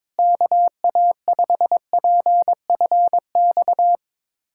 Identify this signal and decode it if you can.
Morse code, heard as a keyed tone, about 22 wpm, 700 Hz. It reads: KA5PFX
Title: USA Callsign DE Callsign - 6 Characters - 22wpm